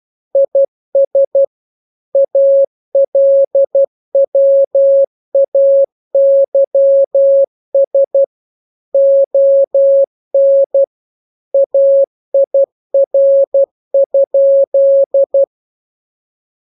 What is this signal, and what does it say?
Morse code, heard as a keyed tone, about 12 wpm, 550 Hz. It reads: IS ALWAYS ON AIR?